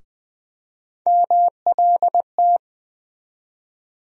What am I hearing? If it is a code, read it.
Morse code, heard as a keyed tone, about 20 wpm, 700 Hz. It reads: MLT
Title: USA Call Sign Postfixes - 3 Letters 20wpm